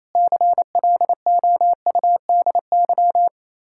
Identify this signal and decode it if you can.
Morse code, heard as a keyed tone, about 32 wpm, 700 Hz. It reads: CLOUDY
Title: Speed-Racing Common QSO Phrases - 28wpm